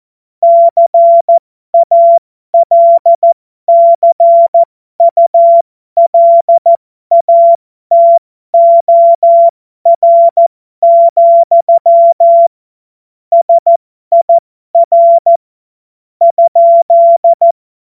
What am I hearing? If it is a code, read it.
Morse code, heard as a keyed tone, about 14 wpm, spaced as for 12 wpm, 680 Hz. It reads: CALCULATOR, SIR ?
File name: Morsekurs / Witze